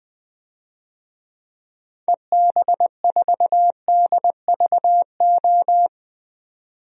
Morse code, heard as a keyed tone, about 20 wpm, 700 Hz. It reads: EB4DVO
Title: DXCC-100 Call Signs - 6 Characters - 20wpm